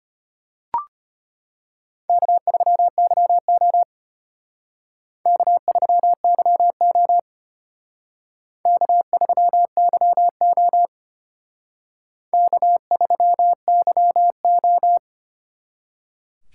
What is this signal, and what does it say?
Morse code, heard as a keyed tone, about 32 wpm, 700 Hz. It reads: K3YO K3YO K3YO K3YO